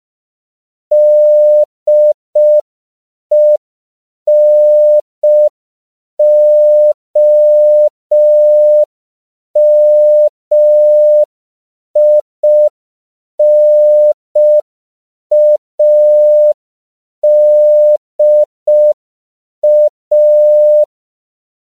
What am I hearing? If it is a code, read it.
Morse code, heard as a keyed tone, about 5 wpm, 600 Hz. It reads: DENOMINADA
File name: CW / 5PPM